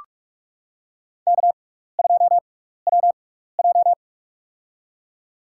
Morse code, heard as a keyed tone, about 44 wpm, spaced as for 15 wpm, 700 Hz. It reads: K2WJ
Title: USA Call Signs - 4 Characters - 6x Character Spacing - 45wpm